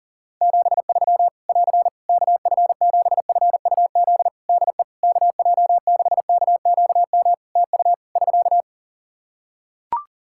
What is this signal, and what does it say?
Morse code, heard as a keyed tone, about 40 wpm, 700 Hz. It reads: 73 <AR> KF7FUZ DE KJ6KQM TU <SK>
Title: Practice QSO - Exchange 4 - Ending - 40wpm